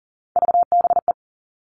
Morse code, heard as a keyed tone, about 40 wpm, 700 Hz. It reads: VBE
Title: All Cut Numbers - 3-Digit 40wpm